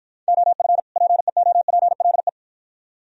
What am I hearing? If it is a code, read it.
Morse code, heard as a keyed tone, about 53 wpm, 700 Hz. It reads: OF PEOPLE